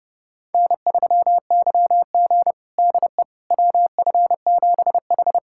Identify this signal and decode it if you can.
Morse code, heard as a keyed tone, about 30 wpm, 700 Hz. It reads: N3YG DE WF7H
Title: USA Callsign DE Callsign - 4 Characters - 30wpm